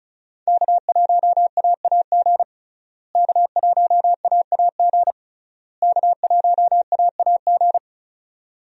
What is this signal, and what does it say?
Morse code, heard as a keyed tone, about 34 wpm, 700 Hz. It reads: K1AAG K1AAG K1AAG
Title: USA Call Signs - 5 Characters - Repeated 3x - 35wpm